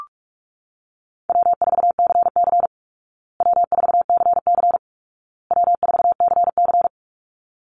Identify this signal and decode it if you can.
Morse code, heard as a keyed tone, about 44 wpm, 700 Hz. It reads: W4CC W4CC W4CC
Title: Easy Call Signs Repeated 3x - 45wpm